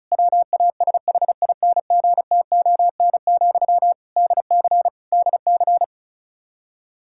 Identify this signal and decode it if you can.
Morse code, heard as a keyed tone, about 36 wpm, 700 Hz. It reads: WASHINGTON, DC DC
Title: Practice QSO - Exchange 1 - Beginning - 35wpm